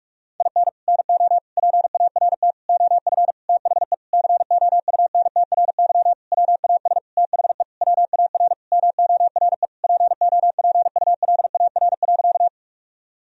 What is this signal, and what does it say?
Morse code, heard as a keyed tone, about 46 wpm, 700 Hz. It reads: IN NO PART OF THE COUNTRY WAS THE WAR MORE POPULAR.
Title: Sentences from Top 1000 Words - Review Part 3 - 45wpm